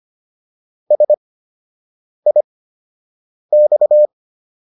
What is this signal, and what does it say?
Morse code, heard as a keyed tone, about 25 wpm, spaced as for 7 wpm, 600 Hz. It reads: SIX